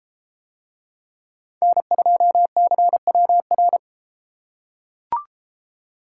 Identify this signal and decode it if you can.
Morse code, heard as a keyed tone, about 31 wpm, 700 Hz. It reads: N2CWR